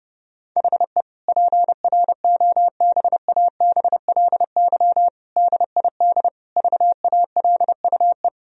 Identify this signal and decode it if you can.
Morse code, heard as a keyed tone, about 30 wpm, 700 Hz. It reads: HE PROBABLY DID VALUE
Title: Sentences from Top 1000 Words - Review Part 4 - 30wpm